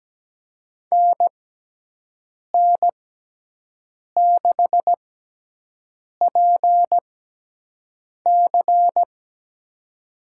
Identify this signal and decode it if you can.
Morse code, heard as a keyed tone, about 17 wpm, spaced as for 6 wpm, 700 Hz. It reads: NN6PC
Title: USA Call Signs - 5 Characters - 6x Character Spacing - 17wpm